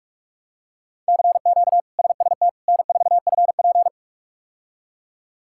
Morse code, heard as a keyed tone, about 43 wpm, 700 Hz. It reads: CQ SST N4FP
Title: K1USN SST QSO Practice - 45wpm